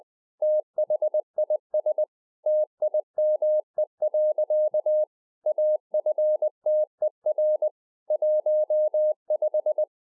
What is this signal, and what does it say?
Morse code, heard as a keyed tone, about 20 wpm, 610 Hz. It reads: THIS TIME. AFTER 15